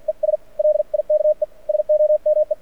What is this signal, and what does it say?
Morse code, heard as a keyed tone, about 47 wpm, 590 Hz. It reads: ES LIKE SOME
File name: archived / CRSnet